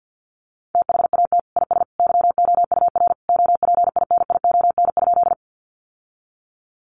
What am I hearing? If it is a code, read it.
Morse code, heard as a keyed tone, about 50 wpm, 700 Hz. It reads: THAT IS YOUR OPINION?
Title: Sentences from Top 1200 Words 50wpm